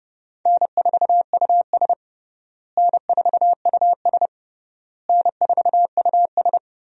Morse code, heard as a keyed tone, about 30 wpm, 700 Hz. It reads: N4US N4US N4US
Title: USA Call Signs - 4 Characters - Repeated 3x - 30wpm